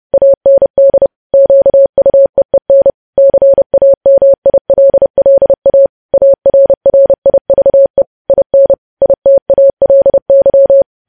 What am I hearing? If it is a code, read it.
Morse code, heard as a keyed tone, about 30 wpm, 550 Hz. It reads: AND QUEEN CAMILLA ARRIVE IN ITALY
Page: news headlines in morse code at 30 wpm